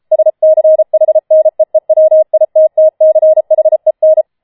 Morse code, heard as a keyed tone, about 33 wpm, 630 Hz. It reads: SCHNEEWITTCHEN